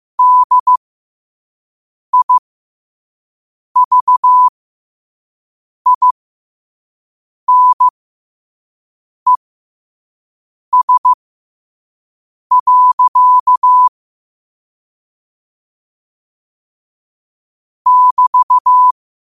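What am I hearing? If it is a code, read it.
Morse code, heard as a keyed tone, about 15 wpm, spaced as for 5 wpm, 1000 Hz. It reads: DIVINES. <BT>
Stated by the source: Morse code quote of the day at 5 WPM.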